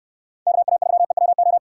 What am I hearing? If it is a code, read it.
Morse code, heard as a keyed tone, about 63 wpm, 700 Hz. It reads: KA3EWO